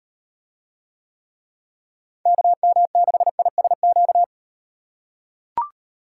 Morse code, heard as a keyed tone, about 38 wpm, 700 Hz. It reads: KM6ISQ